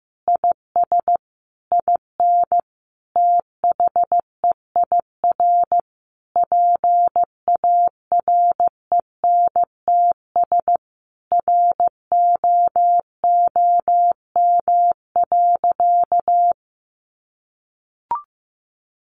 Morse code, heard as a keyed tone, about 15 wpm, 700 Hz. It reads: IS IN THEIR PARENTS ROOM.
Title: Easy Phrases 15wpm